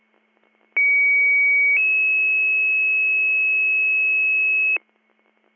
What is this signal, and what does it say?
Two-tone page, heard as a one-second tone, then a three-second tone, about 2250 then 2600 Hz, with noise. It is heard from 0.8 to 4.8 s.